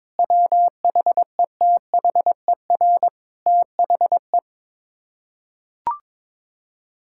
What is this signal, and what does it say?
Morse code, heard as a keyed tone, about 21 wpm, 700 Hz. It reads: WHETHER THE
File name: practice / 22wpm / Sets of 7 Words - 22wpm